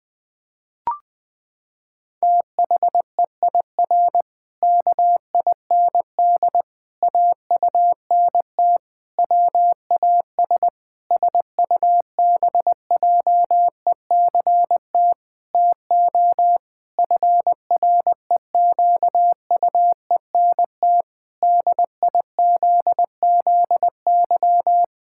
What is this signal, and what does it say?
Morse code, heard as a keyed tone, about 20 wpm, 700 Hz. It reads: THEIR KIND AUNT WAS SUBJECT TO FREQUENT DIZZY
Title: Pangram Challenge v2 - 20wpm